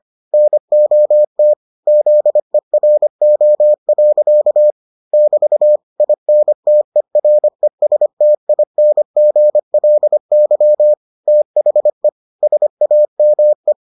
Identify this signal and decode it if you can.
Morse code, heard as a keyed tone, about 25 wpm, 600 Hz. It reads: NOT ZERO. <BT> INTERESTINGLY THE SAME